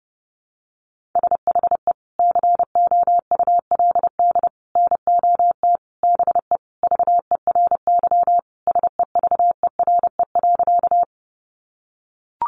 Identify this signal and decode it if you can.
Morse code, heard as a keyed tone, about 30 wpm, 700 Hz. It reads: SHE COULD NOT BE VERY SEVERE.
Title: Sentences from Top 1400 Words 30wpm